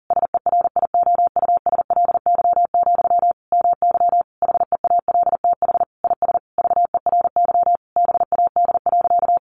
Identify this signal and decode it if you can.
Morse code, heard as a keyed tone, about 40 wpm, 700 Hz. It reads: SERIOUSLY, MY HEALTH IS VERY BAD.
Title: Sentences from Top 1200 Words 40wpm